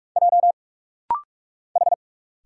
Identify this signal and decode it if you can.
Morse code, heard as a keyed tone, about 46 wpm, 700 Hz. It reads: J H
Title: Single Letters - Rapid-Fire 45wpm